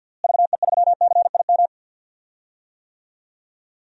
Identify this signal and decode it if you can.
Morse code, heard as a keyed tone, about 52 wpm, 700 Hz. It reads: VE3CIM
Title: DXCC-100 Call Signs - 6 Characters - Repeated 2x - 50wpm